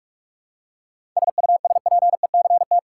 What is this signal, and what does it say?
Morse code, heard as a keyed tone, about 46 wpm, 700 Hz. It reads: SUSPECT
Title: Seven Letter Words - 45wpm